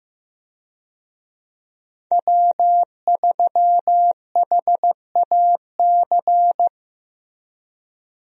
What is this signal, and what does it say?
Morse code, heard as a keyed tone, about 15 wpm, 700 Hz. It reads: W3HAC
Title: USA Call Signs - 5 Characters - Repeated 3x - 15wpm